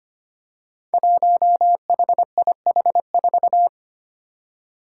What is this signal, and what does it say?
Morse code, heard as a keyed tone, about 25 wpm, 700 Hz. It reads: 1HIH4
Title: Five Challenging Letter-Number Combinations 25wpm